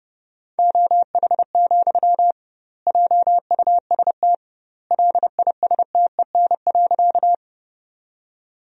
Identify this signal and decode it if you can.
Morse code, heard as a keyed tone, about 30 wpm, 700 Hz. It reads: OH, JUST LISTEN.
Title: 2x Word-Spacing - Sentences from Top 500 Words - 30wpm